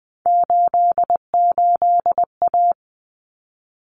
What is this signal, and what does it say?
Morse code, heard as a keyed tone, about 20 wpm, 700 Hz. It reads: 88A
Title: Common Cut Numbers - 3-Digit 20wpm